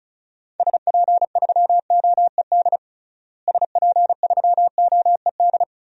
Morse code, heard as a keyed tone, about 34 wpm, 700 Hz. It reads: SP3OED SP3OED
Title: DXCC-100 Call Signs - 6 Characters - Repeated 3x - 35wpm